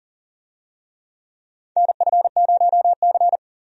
Morse code, heard as a keyed tone, about 40 wpm, 700 Hz. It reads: NF0C
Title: USA Call Signs - 4 Characters - 40wpm